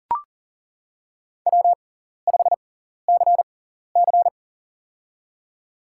Morse code, heard as a keyed tone, about 38 wpm, spaced as for 14 wpm, 700 Hz. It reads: W5CC
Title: USA Call Signs - 6x Character Spacing - 40wpm